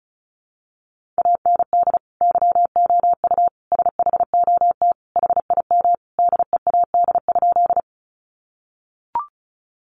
Morse code, heard as a keyed tone, about 34 wpm, 700 Hz. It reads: AND YOU SHOT HIM DEAD?
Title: Sentences from Top 1500 Words 35wpm